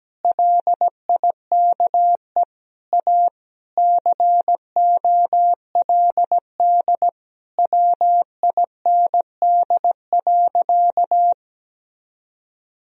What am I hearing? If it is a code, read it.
Morse code, heard as a keyed tone, about 17 wpm, 700 Hz. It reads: LIKE A COLD WIND.
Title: Sentences from Top 1000 Words - Review Part 2 - 17wpm